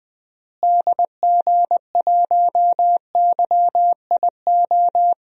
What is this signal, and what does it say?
Morse code, heard as a keyed tone, about 20 wpm, 700 Hz. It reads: DG1YIO
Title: DXCC-100 Call Signs - 6 Characters - Repeated 2x - 20wpm